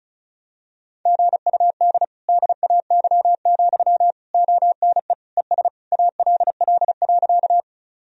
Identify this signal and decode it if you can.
Morse code, heard as a keyed tone, about 34 wpm, 700 Hz. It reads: GUD DAY, ONE ES ALL.
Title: Sentences from Top 200 Words and Abbreviations 35wpm